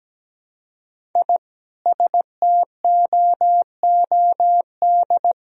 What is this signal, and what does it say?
Morse code, heard as a keyed tone, about 17 wpm, 700 Hz. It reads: I STOOD
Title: Sentences from Top 1000 Words - Review Part 2 - 17wpm